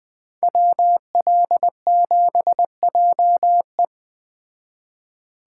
Morse code, heard as a keyed tone, about 20 wpm, 700 Hz. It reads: WL7JE